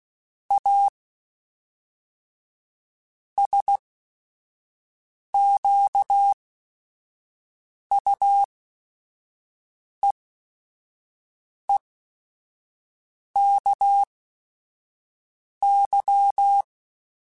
Morse code, heard as a keyed tone, about 16 wpm, spaced as for 5 wpm, 800 Hz. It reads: A SQUEEKY